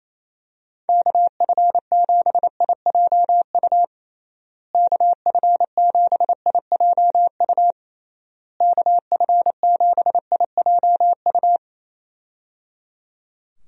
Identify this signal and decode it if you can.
Morse code, heard as a keyed tone, about 28 wpm, 700 Hz. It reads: KF7IJU KF7IJU KF7IJU
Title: USA Call Signs - 6 Characters - Repeated 3x - 28wpm